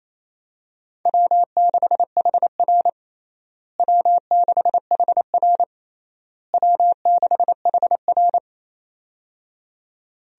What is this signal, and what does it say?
Morse code, heard as a keyed tone, about 28 wpm, 700 Hz. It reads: W6HR W6HR W6HR
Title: USA Call Signs - 4 Characters - Repeated 3x - 28wpm